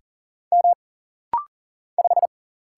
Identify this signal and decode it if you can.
Morse code, heard as a keyed tone, about 40 wpm, 700 Hz. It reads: M 5
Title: Single Letter-Number - Rapid-Fire 40wpm